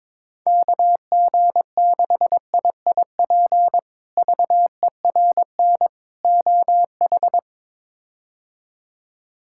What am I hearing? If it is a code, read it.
Morse code, heard as a keyed tone, about 21 wpm, 700 Hz. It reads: KG6IIP VERN OH
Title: K1USN SST QSO Practice - 22wpm